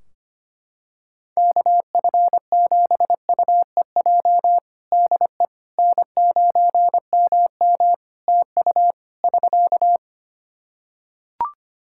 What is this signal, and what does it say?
Morse code, heard as a keyed tone, about 25 wpm, 700 Hz. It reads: KF7UEJ DE N9MM TU <SK>
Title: Practice QSO - Exchange 4 - Ending - 25wpm